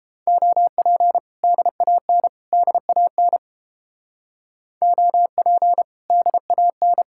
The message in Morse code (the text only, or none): OP DAN DAN OP DAN